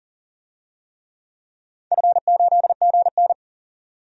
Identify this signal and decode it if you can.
Morse code, heard as a keyed tone, about 40 wpm, 700 Hz. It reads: F8GN
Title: DXCC-100 Call Signs - 4 Characters - 40wpm